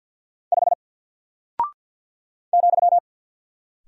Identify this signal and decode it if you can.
Morse code, heard as a keyed tone, about 48 wpm, 700 Hz. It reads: 5 ,
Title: Single Character (Letter:Number:Punctuation:Prosign) Alt 50wpm